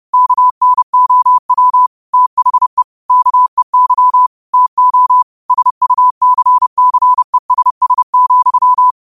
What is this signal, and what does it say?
Morse code, heard as a keyed tone, about 30 wpm, 1000 Hz. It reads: KNOW THE KEY TO SUCCESS,
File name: qotdfiles / 30wpm